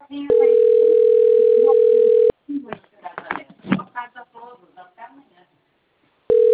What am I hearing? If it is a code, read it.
Ringback tone (North America), 440 + 480 Hz, with speech.
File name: out- - - - - .67307